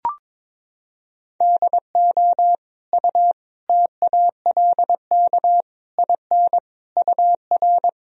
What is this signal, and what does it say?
Morse code, heard as a keyed tone, about 22 wpm, 700 Hz. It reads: DO U TALK IN UR
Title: Sentences from Top 500 Words and Abbreviations 22wpm